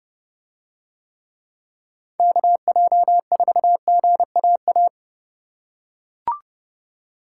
Morse code, heard as a keyed tone, about 29 wpm, 700 Hz. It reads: KJ4GAA